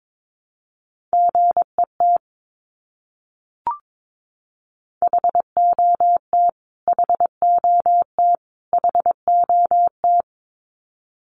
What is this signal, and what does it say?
Morse code, heard as a keyed tone, about 22 wpm, 700 Hz. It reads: GET HOT HOT HOT
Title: Top 200 Words Repeated 22wpm